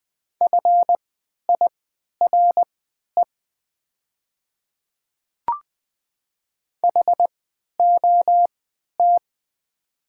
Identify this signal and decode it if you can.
Morse code, heard as a keyed tone, about 20 wpm, spaced as for 11 wpm, 700 Hz. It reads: FIRE HOT
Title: 3x Character Spacing - Top 500 Words - 20wpm